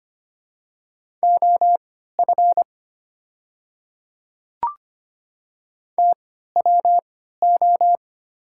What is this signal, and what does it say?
Morse code, heard as a keyed tone, about 25 wpm, spaced as for 14 wpm, 690 Hz. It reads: OF TWO
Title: 3x Character Spacing - Top 100 Words - 25wpm